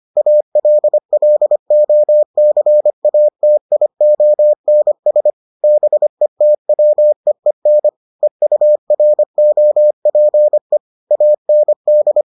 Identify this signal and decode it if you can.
Morse code, heard as a keyed tone, about 25 wpm, 600 Hz. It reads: ALLOCATIONS BETWEEN EUROPE AND